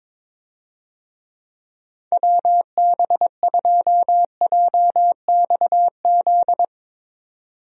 Morse code, heard as a keyed tone, about 22 wpm, 700 Hz. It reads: WB2JXZ
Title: USA Call Signs - 6 Characters - 22wpm